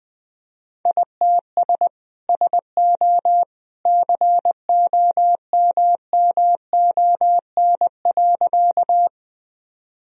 Morse code, heard as a keyed tone, about 21 wpm, 700 Hz. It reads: ITS SO COMMON.
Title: Sentences from Top 1000 Words - Review Part 3 - 20wpm